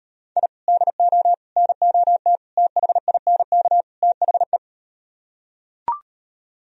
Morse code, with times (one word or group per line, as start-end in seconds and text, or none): 0.4-0.5 I
0.7-1.3 DO
1.6-2.4 NOT
2.6-3.8 THINK
4.0-4.6 THE